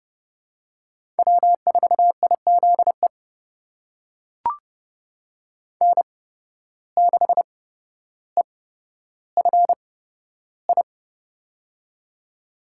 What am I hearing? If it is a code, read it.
Morse code, heard as a keyed tone, about 29 wpm, spaced as for 13 wpm, 700 Hz. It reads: W4IZE N6EFI
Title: USA Call Signs - 5 Characters - 8x Character Spacing - 30wpm